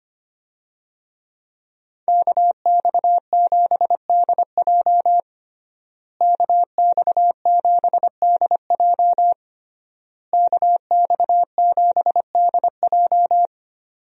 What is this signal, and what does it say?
Morse code, heard as a keyed tone, about 25 wpm, 700 Hz. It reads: KX7DJ KX7DJ KX7DJ
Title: USA Call Signs - 5 Characters - Repeated 3x - 25wpm